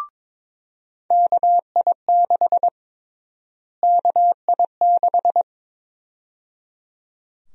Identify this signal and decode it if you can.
Morse code, heard as a keyed tone, about 22 wpm, 700 Hz. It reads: KI6 KI6
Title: USA Call Sign Prefixes - Repeated 2x - 22wpm